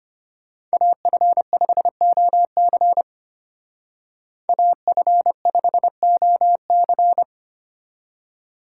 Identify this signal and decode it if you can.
Morse code, heard as a keyed tone, about 30 wpm, 700 Hz. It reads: AF5OC AF5OC
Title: Speed Racing - USA 2x2 Call Signs 25wpm